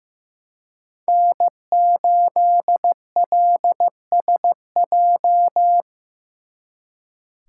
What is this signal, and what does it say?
Morse code, heard as a keyed tone, about 18 wpm, 700 Hz. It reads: N8LSJ
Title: Speed Racing - USA Call Signs 15wpm